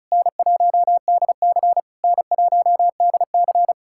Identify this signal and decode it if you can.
Morse code, heard as a keyed tone, about 40 wpm, 700 Hz. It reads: N1DC N1DC